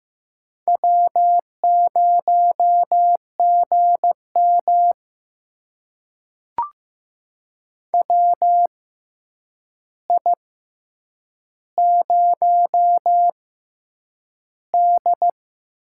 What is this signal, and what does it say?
Morse code, heard as a keyed tone, about 15 wpm, spaced as for 8 wpm, 700 Hz. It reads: W0GM WI0D
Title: USA Call Signs - 4 Characters - 6x Character Spacing - 15wpm